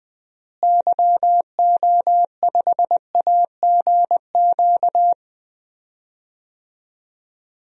Morse code, heard as a keyed tone, about 20 wpm, 700 Hz. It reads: YO5AGQ
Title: DXCC-100 Call Signs - 6 Characters - Repeated 2x - 20wpm